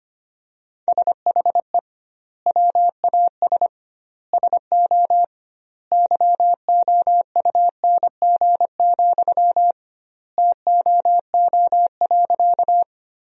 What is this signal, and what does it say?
Morse code, heard as a keyed tone, about 25 wpm, 700 Hz. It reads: SHE WAS SO YOUNG, TOO.
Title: 2x Word-Spacing - Sentences from Top 400 Words - 25wpm